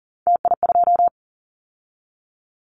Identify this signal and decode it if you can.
Morse code, heard as a keyed tone, about 42 wpm, 700 Hz. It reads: TI2
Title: Callsign Prefixes for Top 100 DXCC - 40wpm